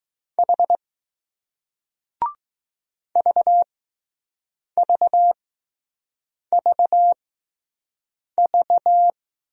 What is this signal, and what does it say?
Morse code, heard as a keyed tone, about 20 wpm, spaced as for 6 wpm, 700 Hz. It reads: H VVVV